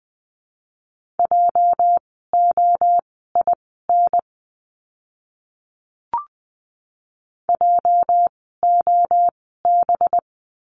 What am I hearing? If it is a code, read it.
Morse code, heard as a keyed tone, about 20 wpm, spaced as for 14 wpm, 700 Hz. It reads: JOIN JOB
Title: 206 Lesson 33f - Letter j - Words 20wpm